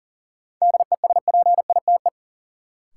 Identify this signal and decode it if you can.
Morse code, heard as a keyed tone, about 40 wpm, 700 Hz. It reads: DESPITE
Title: Sets of 4 Words - New Word Drill - 40wpm